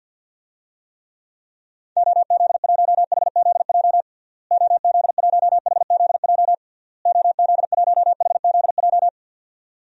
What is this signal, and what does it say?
Morse code, heard as a keyed tone, about 52 wpm, 700 Hz. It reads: OZ1HZJ OZ1HZJ OZ1HZJ
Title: DXCC-100 Call Signs - 6 Characters - Repeated 3x - 50wpm